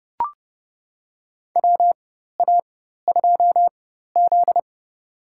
Morse code, heard as a keyed tone, about 30 wpm, spaced as for 14 wpm, 700 Hz. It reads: WA2Z
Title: USA Call Signs - 4x Character Spacing - 30wpm